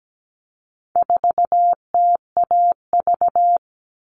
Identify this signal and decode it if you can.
Morse code, heard as a keyed tone, about 17 wpm, 700 Hz. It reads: 4TAV
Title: All Cut Numbers - 4-Digit 17wpm